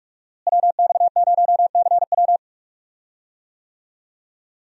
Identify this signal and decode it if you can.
Morse code, heard as a keyed tone, about 41 wpm, 700 Hz. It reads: WX0CW
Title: USA Call Signs - 5 Characters - Repeated 3x - 45wpm